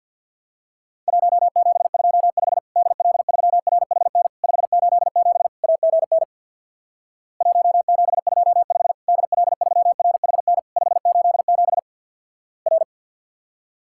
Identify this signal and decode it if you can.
Morse code, heard as a keyed tone, about 48 wpm, 700 Hz. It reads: 1725 DL3RHN 587 AGN 1725 DL3RHN 587 R
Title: Worked All Europe DX Contest - QTC Exchange with Repeats - 50wpm